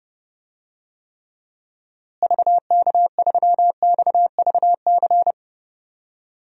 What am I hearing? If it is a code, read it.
Morse code, heard as a keyed tone, about 30 wpm, 700 Hz. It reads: VK3XVC
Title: DXCC-100 Call Signs - 6 Characters - Repeated 3x - 30wpm